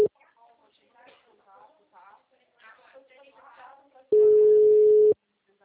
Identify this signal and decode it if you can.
Ringback tone (Europe, 425 Hz), with speech.
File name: out- -2010- - - .22666